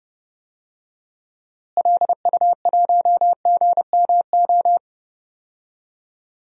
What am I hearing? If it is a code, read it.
Morse code, heard as a keyed tone, about 30 wpm, 700 Hz. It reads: LU1GMO
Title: DXCC-100 Call Signs - 6 Characters - Repeated 2x - 30wpm